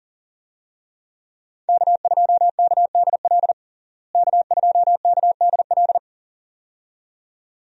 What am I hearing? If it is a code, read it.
Morse code, heard as a keyed tone, about 40 wpm, 700 Hz. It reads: K2KDL K2KDL
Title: USA Call Signs - 5 Characters - Repeated 2x - 40wpm